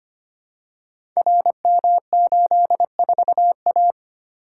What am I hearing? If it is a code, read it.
Morse code, heard as a keyed tone, about 25 wpm, 700 Hz. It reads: RM84A
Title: Code Groups - 5 Letter:Numbers - 25wpm